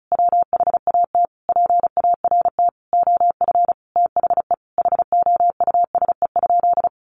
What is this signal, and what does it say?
Morse code, heard as a keyed tone, about 34 wpm, 700 Hz. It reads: WHAT PART OF THE HOUSE?
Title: Sentences from Top 200 Words 35wpm